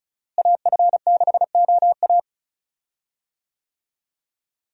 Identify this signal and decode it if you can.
Morse code, heard as a keyed tone, about 34 wpm, 700 Hz. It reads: AF6OA
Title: USA Call Signs - 5 Characters - Repeated 2x - 35wpm